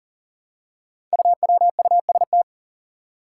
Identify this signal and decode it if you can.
Morse code, heard as a keyed tone, about 40 wpm, 700 Hz. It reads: UWUST
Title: Five Challenging Letter-Number Combinations 40wpm